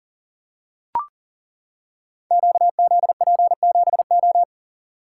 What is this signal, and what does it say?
Morse code, heard as a keyed tone, about 40 wpm, 700 Hz. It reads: QZP7O
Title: Code Groups - 5 Letter:Numbers - 40wpm